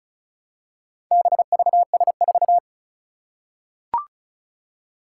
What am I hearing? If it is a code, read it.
Morse code, heard as a keyed tone, about 36 wpm, 700 Hz. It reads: BVS4
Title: Four Challenging Letter-Number Combinations 35wpm